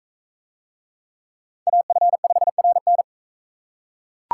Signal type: Morse code

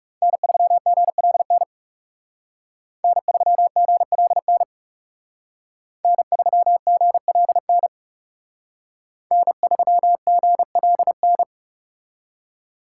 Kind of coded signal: Morse code

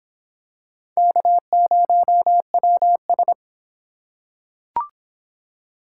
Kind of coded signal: Morse code

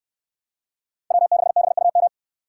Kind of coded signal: Morse code

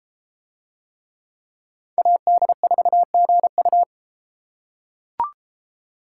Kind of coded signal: Morse code

